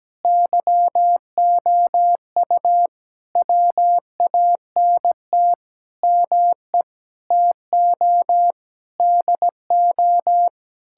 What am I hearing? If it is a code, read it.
Morse code, heard as a keyed tone, about 17 wpm, 700 Hz. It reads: YOU WANT ME TO DO
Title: Sentences from Top 1000 Words - Review Part 6 - 17wpm